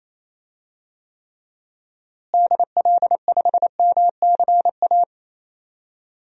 Morse code, heard as a keyed tone, about 28 wpm, 700 Hz. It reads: DL5MCA
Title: DXCC-100 Call Signs - 6 Characters - Repeated 2x - 28wpm